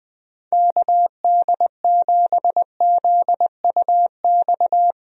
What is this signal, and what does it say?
Morse code, heard as a keyed tone, about 20 wpm, 700 Hz. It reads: KD7ZUX
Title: USA Call Signs - 6 Characters - Repeated 2x - 20wpm